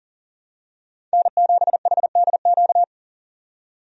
Morse code, heard as a keyed tone, about 40 wpm, 700 Hz. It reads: N7HDQ
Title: USA Call Signs - 5 Characters - 40wpm